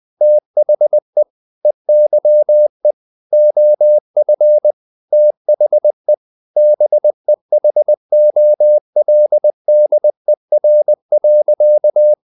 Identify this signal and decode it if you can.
Morse code, heard as a keyed tone, about 20 wpm, 600 Hz. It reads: THE EYE OF THE BEHOLDER.
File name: CW / Practice / EnglishSayings20